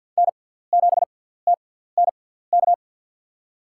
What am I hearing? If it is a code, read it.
Morse code, heard as a keyed tone, about 50 wpm, spaced as for 17 wpm, 700 Hz. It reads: N7TNK
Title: USA Call Signs - 6x Character Spacing - 50wpm